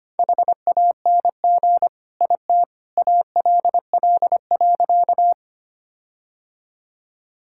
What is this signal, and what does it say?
Morse code, heard as a keyed tone, about 25 wpm, 700 Hz. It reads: HANG IT ALL.
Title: Sentences from Top 1600 Words 25wpm